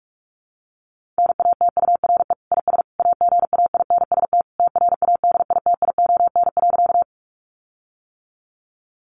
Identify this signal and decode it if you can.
Morse code, heard as a keyed tone, about 44 wpm, 700 Hz. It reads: NATURE IS AGAINST TRADITION.
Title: Sentences from Top 1500 Words 45wpm